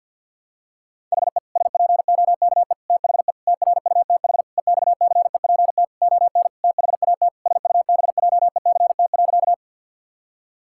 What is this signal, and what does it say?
Morse code, heard as a keyed tone, about 51 wpm, 700 Hz. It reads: HE SPOKE THE TRUTH EXCEPT ON THAT SUBJECT.
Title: Sentences from Top 1600 Words 50wpm